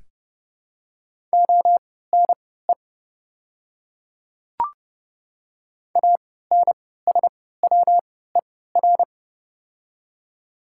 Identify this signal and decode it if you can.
Morse code, heard as a keyed tone, about 30 wpm, spaced as for 17 wpm, 700 Hz. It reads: ONE ANSWER
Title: 3x Character Spacing - Top 300 Words - 30wpm